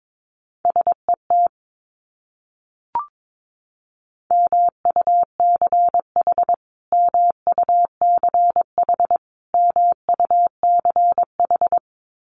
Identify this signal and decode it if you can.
Morse code, heard as a keyed tone, about 22 wpm, 700 Hz. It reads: SET MUCH MUCH MUCH
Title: Top 200 Words Repeated 22wpm